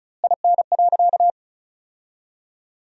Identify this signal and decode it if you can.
Morse code, heard as a keyed tone, about 36 wpm, 700 Hz. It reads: IN.
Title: 2x Word-Spacing - Sentences from Top 400 Words - 35wpm